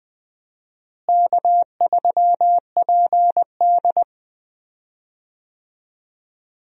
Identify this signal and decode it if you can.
Morse code, heard as a keyed tone, about 20 wpm, 700 Hz. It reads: K3PD
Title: USA Call Signs - 4 Characters - Repeated 3x - 20wpm